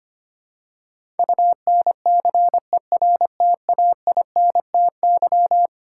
Morse code, heard as a keyed tone, about 25 wpm, 700 Hz. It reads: UNCERTAINTY
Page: morse-code-ninja-practice-25wpm